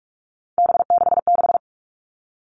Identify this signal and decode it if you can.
Morse code, heard as a keyed tone, about 44 wpm, 700 Hz. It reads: B66
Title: All Cut Numbers - 3-Digit 45wpm